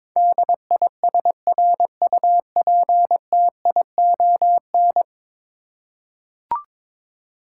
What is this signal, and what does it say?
Morse code, heard as a keyed tone, about 22 wpm, 700 Hz. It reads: DISRUPTION
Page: morse-code-ninja-practice-22wpm